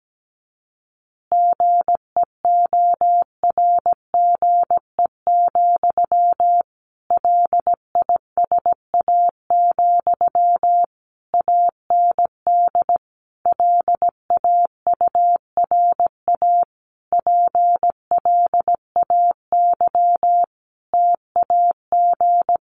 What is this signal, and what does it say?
Morse code, heard as a keyed tone, about 17 wpm, 700 Hz. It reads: GEORGE, LISA, AND LAURA PLAY TAG
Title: Additional Easy Phrases 17wpm